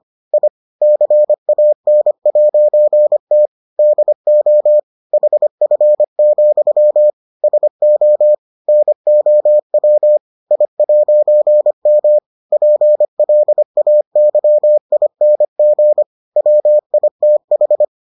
Morse code, heard as a keyed tone, about 25 wpm, 600 Hz. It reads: I CAN'T DO HF, SO NOW I'M PLAYING WITH